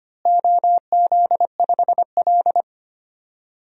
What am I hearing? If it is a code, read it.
Morse code, heard as a keyed tone, about 25 wpm, 700 Hz. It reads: OZ5L
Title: DXCC-100 Call Signs - 4 Characters - Repeated 3x - 25wpm